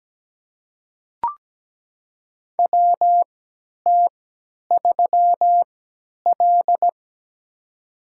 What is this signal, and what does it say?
Morse code, heard as a keyed tone, about 17 wpm, spaced as for 10 wpm, 700 Hz. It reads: WT3L
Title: USA Call Signs - 4 Characters - 3x Character Spacing - 17wpm